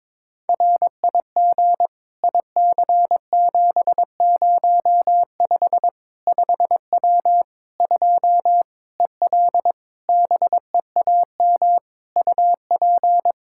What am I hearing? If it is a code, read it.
Morse code, heard as a keyed tone, about 22 wpm, 700 Hz. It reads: RIG IC705 5W 2 EL BEAM UP
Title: K1USN SST QSO Practice 22wpm